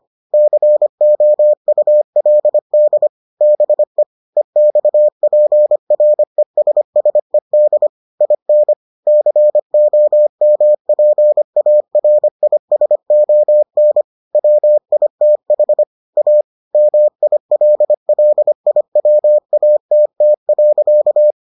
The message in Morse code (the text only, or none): COULD BE EXPRESSED IN COMPARISON WITH A MILLIWATT.